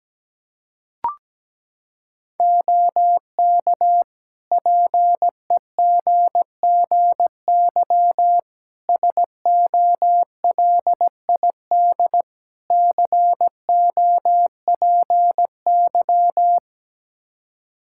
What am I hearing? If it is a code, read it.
Morse code, heard as a keyed tone, about 17 wpm, 700 Hz. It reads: OK PEGGY SOLID COPY
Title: Practice QSO - Exchange 3 - More Info - 17wpm